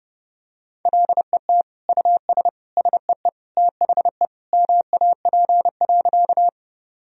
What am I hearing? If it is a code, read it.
Morse code, heard as a keyed tone, about 32 wpm, 700 Hz. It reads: LET US SEE THE MAP.
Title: Sentences from Top 1000 Words - Review Part 2 - 30wpm